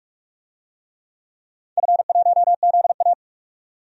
Morse code, heard as a keyed tone, about 43 wpm, 700 Hz. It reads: F1ZA
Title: DXCC-100 Call Signs - 4 Characters - 45wpm